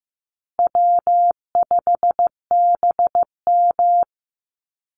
Morse code, heard as a keyed tone, about 15 wpm, 700 Hz. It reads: W5BM
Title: Easy Call Signs Repeated 3x - 15wpm